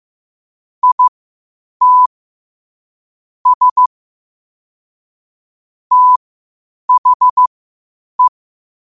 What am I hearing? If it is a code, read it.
Morse code, heard as a keyed tone, about 15 wpm, spaced as for 8 wpm, 1000 Hz. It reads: IT S THE
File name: qotdfiles / 8wpm